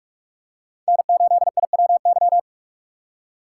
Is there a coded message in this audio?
Yes: Morse code from 0.9 to 2.4 s, heard as a keyed tone, about 43 wpm, 700 Hz. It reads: N8IWY